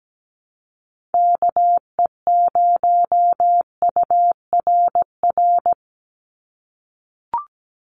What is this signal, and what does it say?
Morse code, heard as a keyed tone, about 17 wpm, 700 Hz. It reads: KE0URR
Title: USA Call Signs 17wpm